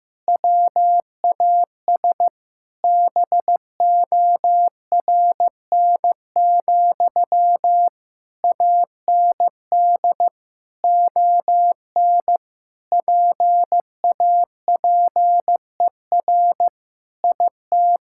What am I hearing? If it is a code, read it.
Morse code, heard as a keyed tone, about 15 wpm, 700 Hz. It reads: WAS BORN, AND ON PAPER IT
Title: Sentences from Top 1600 Words 15wpm